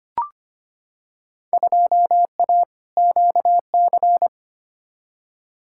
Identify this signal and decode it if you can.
Morse code, heard as a keyed tone, about 25 wpm, 700 Hz. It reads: 2A QC
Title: Field Day Exchanges - 25wpm